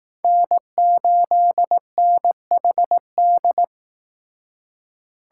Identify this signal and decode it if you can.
Morse code, heard as a keyed tone, about 18 wpm, 700 Hz. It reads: N8NHD
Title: Speed Racing - USA 1x3 Call Signs 15wpm